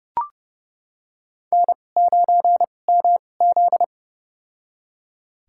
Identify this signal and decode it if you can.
Morse code, heard as a keyed tone, about 31 wpm, spaced as for 22 wpm, 700 Hz. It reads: N9MZ
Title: USA Call Signs - 2x Character Spacing - 30wpm